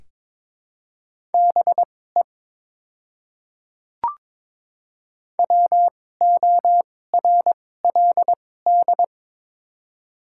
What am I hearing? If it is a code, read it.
Morse code, heard as a keyed tone, about 21 wpm, spaced as for 16 wpm, 700 Hz. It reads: BE WORLD
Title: 2x Character Spacing - Top 500 Words - 22wpm